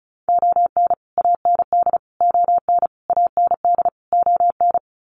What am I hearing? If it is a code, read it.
Morse code, heard as a keyed tone, about 34 wpm, 700 Hz. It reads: ON AND ON AND ON
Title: Sets of 5 Words Encore2 35wpm